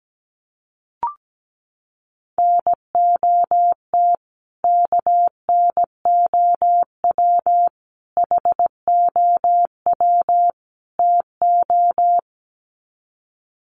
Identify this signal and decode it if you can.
Morse code, heard as a keyed tone, about 17 wpm, 700 Hz. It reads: NOT KNOW HOW TO
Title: Sets of 4 Words 17wpm